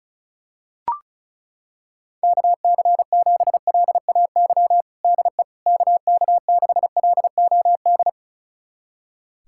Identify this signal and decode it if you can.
Morse code, heard as a keyed tone, about 34 wpm, 690 Hz. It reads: KC7LAY DE KK6LOD
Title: USA Callsign DE Callsign - 6 Characters - 35wpm